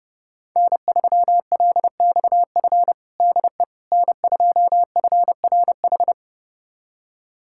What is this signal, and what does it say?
Morse code, heard as a keyed tone, about 31 wpm, 700 Hz. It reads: N3LXF DE N2FRH
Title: USA Callsign DE Callsign - 5 Characters - 30wpm